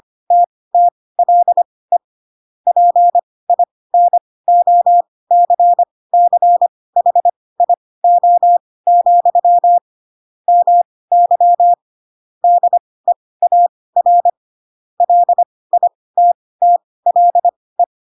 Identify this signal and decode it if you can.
Morse code, heard as a keyed tone, about 25 wpm, spaced as for 18 wpm, 700 Hz. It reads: TTLE PINOCCHIO, MY DEAR LITTLE
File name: Pinocchio / Pinocchio ch7 12 25 ik1hge